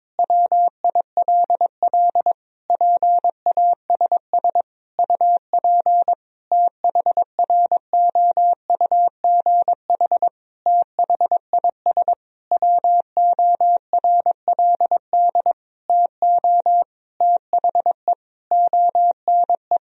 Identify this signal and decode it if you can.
Morse code, heard as a keyed tone, about 22 wpm, 700 Hz. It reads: WILL PASS UP THROUGH THIS WORLD TO THE ONE
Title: Sentences from Top 1000 Words - Review Part 2 - 22wpm